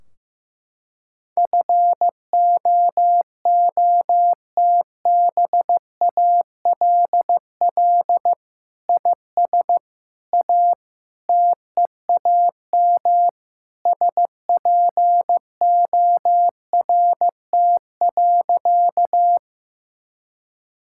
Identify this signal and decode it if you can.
Morse code, heard as a keyed tone, about 16 wpm, 700 Hz. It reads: FOOTBALL IS A TEAM SPORT.
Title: Sentences from Top 1600 Words 15wpm